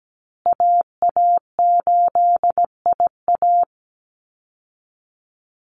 Morse code, heard as a keyed tone, about 17 wpm, 700 Hz. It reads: AA8IA
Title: Call Signs Repeated 2x - 17wpm